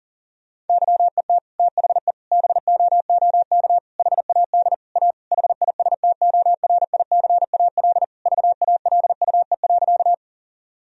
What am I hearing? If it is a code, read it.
Morse code, heard as a keyed tone, about 37 wpm, 700 Hz. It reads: YET THE BOOK HAD A HISTORICAL VALUE.
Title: Sentences from Top 1000 Words - Review Part 4 - 40wpm